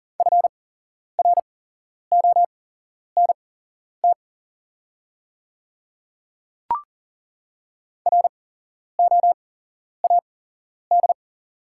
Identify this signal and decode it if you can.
Morse code, heard as a keyed tone, about 40 wpm, spaced as for 11 wpm, 700 Hz. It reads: FRONT ROAD
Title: 8x Character Spacing - Top 500 Words - 40wpm